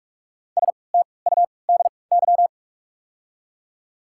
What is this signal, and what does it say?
Morse code, heard as a keyed tone, about 48 wpm, spaced as for 26 wpm, 690 Hz. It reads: STUDY
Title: 3x Character Spacing - Top 300 Words - 45wpm